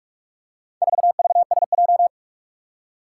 Morse code, heard as a keyed tone, about 44 wpm, 700 Hz. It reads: 4VSJ
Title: Four Challenging Letter-Number Combinations 45wpm